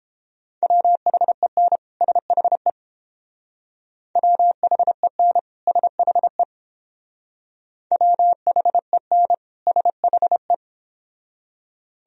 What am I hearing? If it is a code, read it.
Morse code, heard as a keyed tone, about 29 wpm, 700 Hz. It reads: WHEN SHE WHEN SHE WHEN SHE